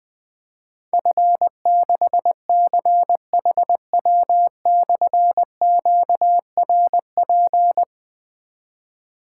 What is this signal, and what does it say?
Morse code, heard as a keyed tone, about 20 wpm, 700 Hz. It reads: F6CHW/QRP
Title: Difficult Call Signs 20wpm